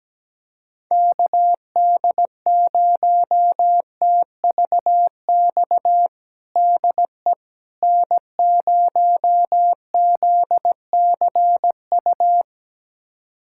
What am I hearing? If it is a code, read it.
Morse code, heard as a keyed tone, about 17 wpm, 700 Hz. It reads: KD0TVX DE N0ZCU
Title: USA Callsign DE Callsign 17wpm